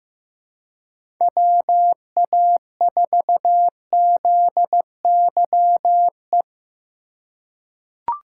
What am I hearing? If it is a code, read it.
Morse code, heard as a keyed tone, about 16 wpm, 700 Hz. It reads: WA4ZYE
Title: USA Call Signs - 6 Characters - 2x Character Spacing - 15wpm